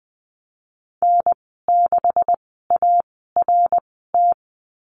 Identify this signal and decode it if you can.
Morse code, heard as a keyed tone, about 20 wpm, spaced as for 15 wpm, 700 Hz. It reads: N6ART
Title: 154 Lesson 25g - Letter p - Call Signs 20wpm